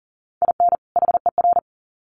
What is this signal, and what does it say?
Morse code, heard as a keyed tone, about 40 wpm, 700 Hz. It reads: IN HER
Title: Sets of 2 Words 40wpm